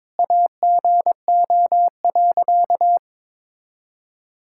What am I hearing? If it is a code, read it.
Morse code, heard as a keyed tone, about 22 wpm, 700 Hz. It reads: AGO.
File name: Sentences with 7-Word Phrases - 22wpm